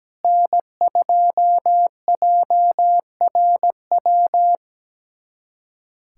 Morse code, heard as a keyed tone, about 17 wpm, 700 Hz. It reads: N2JRW
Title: USA Callsign DE Callsign - 5 Characters - 17wpm